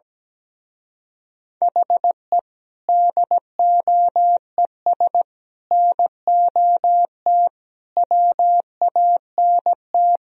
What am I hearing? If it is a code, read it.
Morse code, heard as a keyed tone, about 17 wpm, 700 Hz. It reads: HE DOES NOT WANT
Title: Speed-Racing - Sets of 5 Words 17wpm